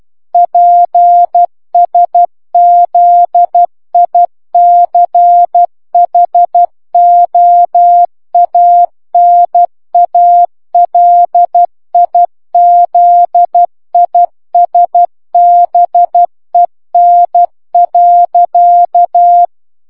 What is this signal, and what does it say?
Morse code, heard as a keyed tone, about 12 wpm, 700 Hz. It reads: PSZICHOANALIZISBEN.